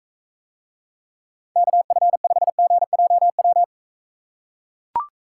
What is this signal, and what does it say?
Morse code, heard as a keyed tone, about 43 wpm, 700 Hz. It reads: KF5GJW